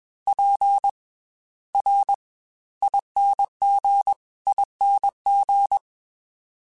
Morse code, heard as a keyed tone, about 21 wpm, 800 Hz. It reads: P R ING ING
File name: Special INGs / Special INGs-mp3s / Slow ing @20